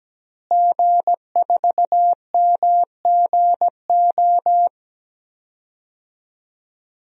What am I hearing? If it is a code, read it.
Morse code, heard as a keyed tone, about 17 wpm, 700 Hz. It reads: G4MGO
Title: DXCC-100 Call Signs - 5 Characters - Repeated 2x - 17wpm